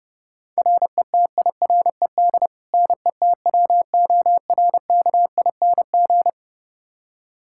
Morse code, heard as a keyed tone, about 32 wpm, 700 Hz. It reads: RETIRED NETWORKING
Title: K1USN SST QSO Practice 30wpm